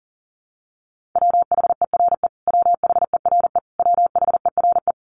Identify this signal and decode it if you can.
Morse code, heard as a keyed tone, about 40 wpm, 700 Hz. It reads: WHERE WHERE WHERE
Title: Top 200 Words Repeated 40wpm